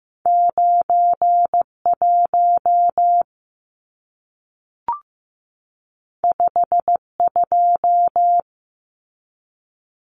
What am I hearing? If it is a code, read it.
Morse code, heard as a keyed tone, about 15 wpm, 700 Hz. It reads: 91 52
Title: Numbers - 2-Digit 15wpm